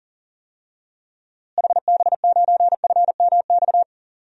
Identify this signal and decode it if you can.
Morse code, heard as a keyed tone, about 37 wpm, 700 Hz. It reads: HB9FMX
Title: DXCC-100 Call Signs - 6 Characters - Repeated 2x - 40wpm